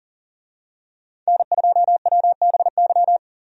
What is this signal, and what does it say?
Morse code, heard as a keyed tone, about 40 wpm, 700 Hz. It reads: N2WBY
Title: USA Call Signs - 5 Characters - 40wpm